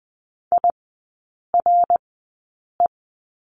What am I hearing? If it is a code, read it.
Morse code, heard as a keyed tone, about 20 wpm, spaced as for 8 wpm, 700 Hz. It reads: IRE
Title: 053 Lesson 10e - Letter r - Three Characters 20wpm